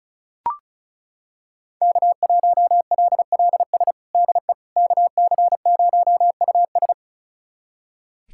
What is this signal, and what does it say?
Morse code, heard as a keyed tone, about 36 wpm, 700 Hz. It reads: K1LLS DE KC0US
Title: USA Callsign DE Callsign - 5 Characters - 35wpm